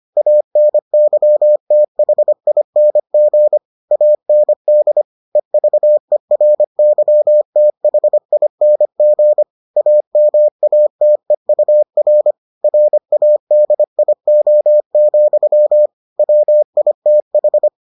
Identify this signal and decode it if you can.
Morse code, heard as a keyed tone, about 25 wpm, 600 Hz. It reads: ANYTHING AND EVERYTHING AMATEUR RADIO, WITH